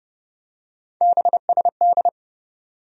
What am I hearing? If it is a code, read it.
Morse code, heard as a keyed tone, about 31 wpm, 700 Hz. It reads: BSD
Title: USA Call Sign Postfixes - 3 Letters 30wpm